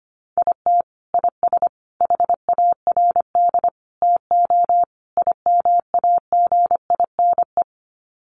Morse code, heard as a keyed tone, about 25 wpm, 700 Hz. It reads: IT IS HARD TO IMAGINE
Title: Sets of 5 Words Encore2 25wpm